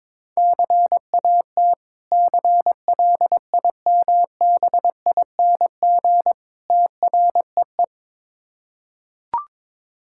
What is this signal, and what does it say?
Morse code, heard as a keyed tone, about 22 wpm, 690 Hz. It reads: CAT CLIMBING TREE
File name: Difficult Words with Hints - 22wpm